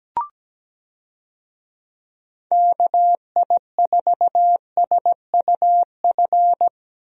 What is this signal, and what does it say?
Morse code, heard as a keyed tone, about 17 wpm, 700 Hz. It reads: KI4SUF
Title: USA Call Signs - 6 Characters - Repeated 2x - 17wpm